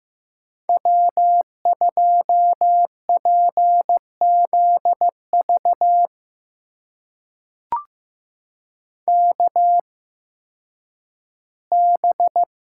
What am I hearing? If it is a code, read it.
Morse code, heard as a keyed tone, about 15 wpm, spaced as for 10 wpm, 700 Hz. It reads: W2PZV KB
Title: USA Call Signs - 8x Character Spacing - 15wpm